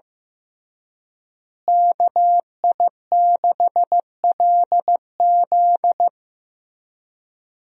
Morse code, heard as a keyed tone, about 15 wpm, 700 Hz. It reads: KI6LZ
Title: USA Call Signs - 5 Characters - Repeated 3x - 15wpm